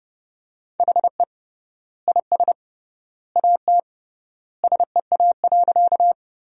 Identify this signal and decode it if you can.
Morse code, heard as a keyed tone, about 29 wpm, 700 Hz. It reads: HE IS AT SEA.
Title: 3x Word-Spacing - Sentences from Top 300 Words - 30wpm